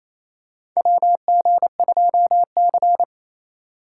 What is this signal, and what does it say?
Morse code, heard as a keyed tone, about 28 wpm, 700 Hz. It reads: WG2C
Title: USA Call Signs - 4 Characters - 3x Character Spacing - 28wpm